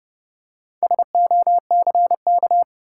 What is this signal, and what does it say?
Morse code, heard as a keyed tone, about 30 wpm, 700 Hz. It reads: SOCK
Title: Four Letter Words - 30wpm